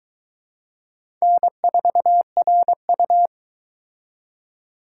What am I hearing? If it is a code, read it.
Morse code, heard as a keyed tone, about 22 wpm, 700 Hz. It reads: N4RU